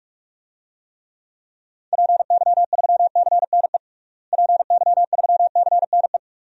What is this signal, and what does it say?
Morse code, heard as a keyed tone, about 46 wpm, 700 Hz. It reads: PY3CNE PY3CNE
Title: DXCC-100 Call Signs - 6 Characters - Repeated 3x - 45wpm